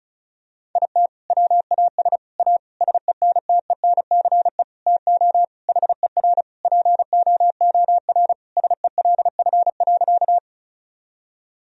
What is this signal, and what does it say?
Morse code, heard as a keyed tone, about 36 wpm, 700 Hz. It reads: IT WAS A SENTENCE TO HER POOR SELF.
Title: Sentences from Top 1000 Words - Review Part 1 - 35wpm